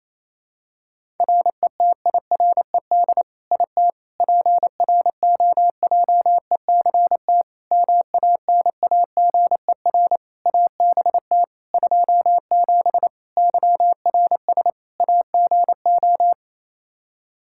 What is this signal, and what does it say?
Morse code, heard as a keyed tone, about 28 wpm, 700 Hz. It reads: RETIRED IT PROJECT MANAGER ABT 27 YRS AGO
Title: Practice QSO - Exchange 3 - More Info - 28wpm